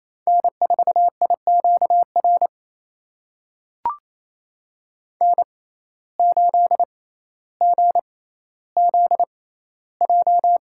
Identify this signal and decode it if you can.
Morse code, heard as a keyed tone, about 28 wpm, spaced as for 14 wpm, 700 Hz. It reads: N4IQR N8GZJ
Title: USA Call Signs - 5 Characters - 6x Character Spacing - 28wpm